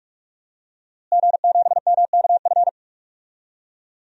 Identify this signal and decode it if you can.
Morse code, heard as a keyed tone, about 44 wpm, 700 Hz. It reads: G7MKF
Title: DXCC-100 Call Signs - 5 Characters - Repeated 2x - 45wpm